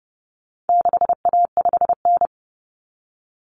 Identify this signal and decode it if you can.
Morse code, heard as a keyed tone, about 30 wpm, 700 Hz. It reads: 6A5N
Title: Common Cut Numbers - 4-Digit 30wpm